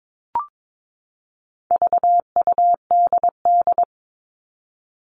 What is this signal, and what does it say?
Morse code, heard as a keyed tone, about 22 wpm, 700 Hz. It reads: VUDD
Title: All Cut Numbers - 4-Digit 22wpm